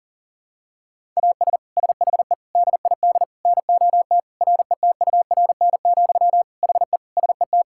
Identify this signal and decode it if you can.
Morse code, heard as a keyed tone, about 37 wpm, 700 Hz. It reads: AS SHE DID NOT RETURN, HE SET
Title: Sentences from Top 1000 Words - Review Part 5 - 40wpm